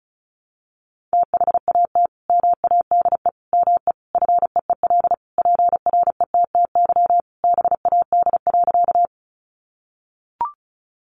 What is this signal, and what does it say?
Morse code, heard as a keyed tone, about 34 wpm, 700 Hz. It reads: THAT MADE ME FEEL PRETTY BAD.
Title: Sentences from Top 1000 Words 35wpm